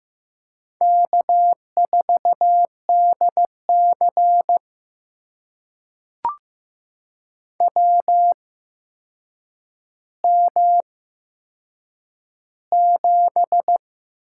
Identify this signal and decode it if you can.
Morse code, heard as a keyed tone, about 15 wpm, spaced as for 7 wpm, 700 Hz. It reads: K4DC WM7
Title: USA Call Signs - 4 Characters - 8x Character Spacing - 15wpm